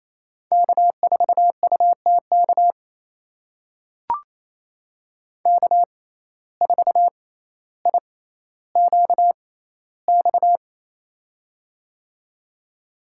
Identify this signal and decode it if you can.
Morse code, heard as a keyed tone, about 28 wpm, spaced as for 14 wpm, 700 Hz. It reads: K4UTK K4IQX
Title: USA Call Signs - 5 Characters - 6x Character Spacing - 28wpm